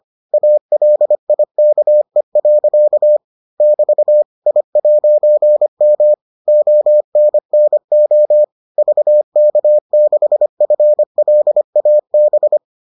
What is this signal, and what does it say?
Morse code, heard as a keyed tone, about 25 wpm, 600 Hz. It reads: ALIKE. <BT> I'M ONNO VK6FLAB